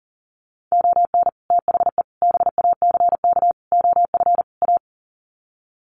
Morse code, heard as a keyed tone, about 40 wpm, 700 Hz. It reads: ON THE BACK OF A
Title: Sets of 5 Words Encore2 40wpm